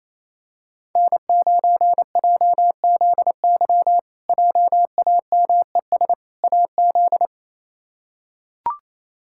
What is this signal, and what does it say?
Morse code, heard as a keyed tone, about 28 wpm, 700 Hz. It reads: N9JZY JAMES AZ
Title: K1USN SST QSO Practice - 28wpm